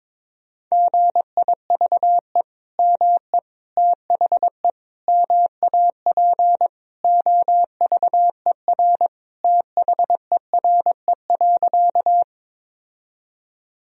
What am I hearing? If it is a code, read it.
Morse code, heard as a keyed tone, about 22 wpm, 700 Hz. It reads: GIVE ME THE MAP OVER THERE.
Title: Sentences from Top 1000 Words - Review Part 2 - 22wpm